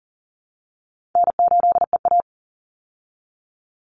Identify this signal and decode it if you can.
Morse code, heard as a keyed tone, about 40 wpm, 700 Hz. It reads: N8EA
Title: Easy Call Signs Repeated 2x - 40wpm